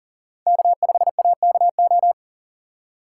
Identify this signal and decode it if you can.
Morse code, heard as a keyed tone, about 40 wpm, 700 Hz. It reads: K5AKO
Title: USA Call Signs - 5 Characters - 4x Character Spacing - 40wpm